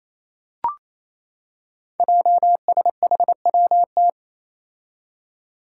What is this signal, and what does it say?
Morse code, heard as a keyed tone, about 29 wpm, 700 Hz. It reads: JSHWT
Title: Five Challenging Letter-Number Combinations 28wpm